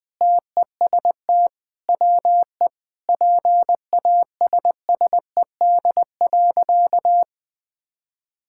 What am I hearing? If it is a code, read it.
Morse code, heard as a keyed tone, about 21 wpm, 700 Hz. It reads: TEST WE PASSED.
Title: Sentences with 4-Word Phrases - 20wpm